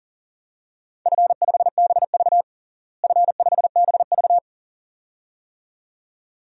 Morse code, heard as a keyed tone, about 37 wpm, 700 Hz. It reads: F5BV F5BV
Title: DXCC-100 Call Signs - 4 Characters - Repeated 2x - 40wpm